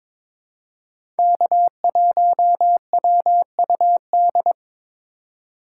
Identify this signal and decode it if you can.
Morse code, heard as a keyed tone, about 22 wpm, 700 Hz. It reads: K1WUD
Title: USA Call Signs - 5 Characters - 3x Character Spacing - 22wpm